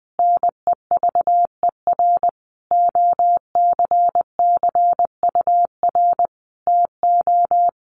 Morse code, heard as a keyed tone, about 19 wpm, 700 Hz. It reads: NEVER OCCUR TO
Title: Sentences from Top 900 Words 20wpm